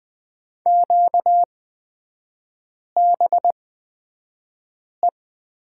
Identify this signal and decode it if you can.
Morse code, heard as a keyed tone, about 20 wpm, spaced as for 5 wpm, 700 Hz. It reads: QBE
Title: 236 Lesson 38e - Letter q - Three Characters 20wpm